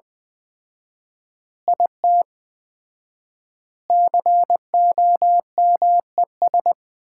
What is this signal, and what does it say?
Morse code, heard as a keyed tone, about 20 wpm, 700 Hz. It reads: IT COMES
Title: 4x Word-Spacing - Sets of 5 Words - 20wpm